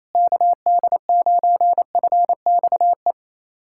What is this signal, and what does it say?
Morse code, heard as a keyed tone, about 28 wpm, 700 Hz. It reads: KD9FXE
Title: SOTA QSO Practice 28wpm